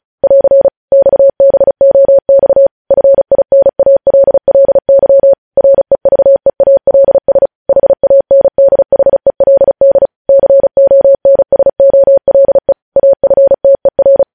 Morse code, heard as a keyed tone, about 36 wpm, 550 Hz. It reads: <AR> XBOX FINALLY REVEALS HANDHELD CONSOLE AFTER
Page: news headlines in morse code at 35 wpm